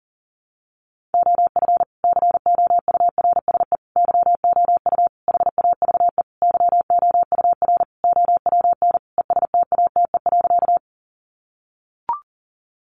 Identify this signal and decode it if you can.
Morse code, heard as a keyed tone, about 40 wpm, 700 Hz. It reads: OF COURSE YOU HAVE YOUR OWN ESTATE.
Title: Sentences from Top 1200 Words 40wpm